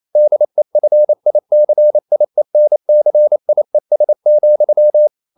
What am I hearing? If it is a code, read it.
Morse code, heard as a keyed tone, about 28 wpm, 600 Hz. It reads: DEFICIENCIES,
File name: CW / Practice / EnglishQuotations0228